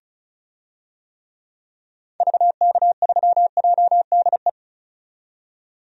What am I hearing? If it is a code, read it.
Morse code, heard as a keyed tone, about 36 wpm, 690 Hz. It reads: VK3JDE
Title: DXCC-100 Call Signs - 6 Characters - 35wpm